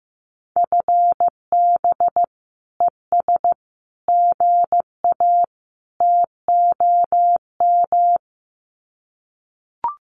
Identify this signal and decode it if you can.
Morse code, heard as a keyed tone, about 15 wpm, 700 Hz. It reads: FB ES GA TOM
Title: Common QSO Phrases Encore 15wpm